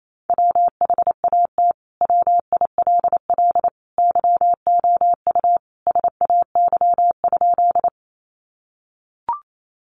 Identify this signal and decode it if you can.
Morse code, heard as a keyed tone, about 28 wpm, 700 Hz. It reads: WHAT WILL YOU SAY?
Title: Sentences from Top 200 Words 28wpm